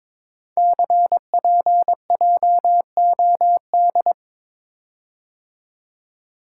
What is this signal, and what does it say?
Morse code, heard as a keyed tone, about 22 wpm, 700 Hz. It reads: CPJOD
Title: Code Groups - 5 Letters - 22wpm